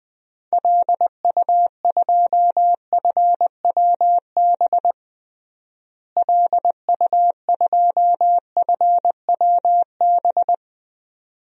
Morse code, heard as a keyed tone, about 20 wpm, 700 Hz. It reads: LU2FWB LU2FWB
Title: DXCC-100 Call Signs - 6 Characters - Repeated 2x - 20wpm